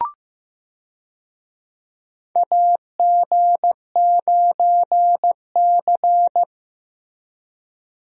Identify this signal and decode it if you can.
Morse code, heard as a keyed tone, about 15 wpm, 700 Hz. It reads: AG9C
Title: USA Call Signs - 4 Characters - Repeated 2x - 15wpm